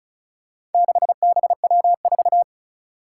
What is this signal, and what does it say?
Morse code, heard as a keyed tone, about 36 wpm, 700 Hz. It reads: 6BW4
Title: Four Challenging Letter-Number Combinations 35wpm